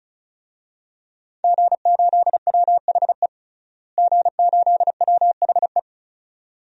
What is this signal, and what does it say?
Morse code, heard as a keyed tone, about 36 wpm, 700 Hz. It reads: G8WHE G8WHE
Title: DXCC-100 Call Signs - 5 Characters - Repeated 2x - 35wpm